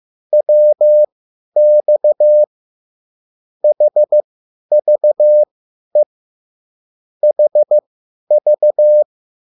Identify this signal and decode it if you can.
Morse code, heard as a keyed tone, about 15 wpm, spaced as for 10 wpm, 600 Hz. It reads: WX HVE HV
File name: QSL / CW lesson